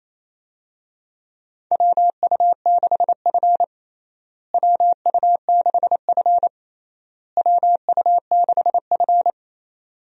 Morse code, heard as a keyed tone, about 28 wpm, 700 Hz. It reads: WU6F WU6F WU6F
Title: USA Call Signs - 4 Characters - Repeated 3x - 28wpm